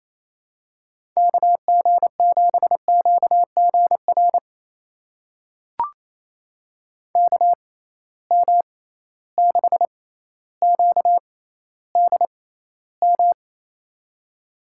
Morse code, heard as a keyed tone, about 27 wpm, spaced as for 14 wpm, 700 Hz. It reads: KG7QGR KM6QDM
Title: USA Call Signs - 6 Characters - 6x Character Spacing - 28wpm